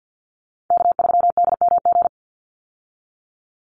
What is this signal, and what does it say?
Morse code, heard as a keyed tone, about 50 wpm, 700 Hz. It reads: K3DMG
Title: Call Signs Repeated 2x - 50wpm